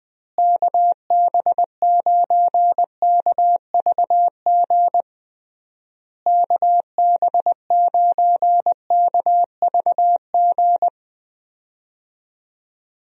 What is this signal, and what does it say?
Morse code, heard as a keyed tone, about 19 wpm, 700 Hz. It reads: KB9KVG KB9KVG
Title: USA Call Signs - 6 Characters - Repeated 3x - 20wpm